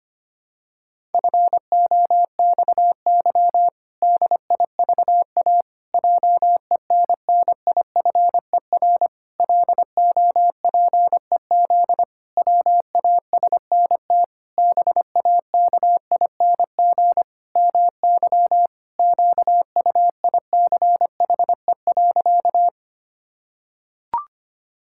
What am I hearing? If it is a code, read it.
Morse code, heard as a keyed tone, about 25 wpm, 700 Hz. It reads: FOXY DIVA JENNIFER LOPEZ WASNT BAKING MY QUICHE.
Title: Pangram Challenge v2 - 25wpm